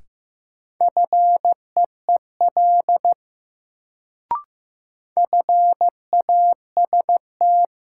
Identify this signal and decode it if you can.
Morse code, heard as a keyed tone, about 15 wpm, 700 Hz. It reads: FEEL FAST
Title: Sentences from Top 400 Words - New Word Drill - 15wpm